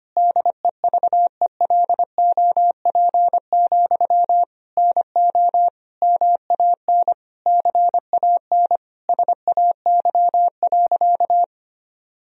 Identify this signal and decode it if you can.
Morse code, heard as a keyed tone, about 25 wpm, 700 Hz. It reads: DEVELOP, NO MAN CAN SAY.
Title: Sentences from Top 1000 Words - Review Part 2 - 25wpm